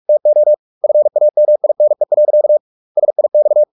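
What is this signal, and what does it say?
Morse code, heard as a keyed tone, about 44 wpm, 600 Hz. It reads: TO FAMINE. SIX
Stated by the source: Morse practice at 45 wpm, the novel Never by Ken Follett in English, from CW med Gnister.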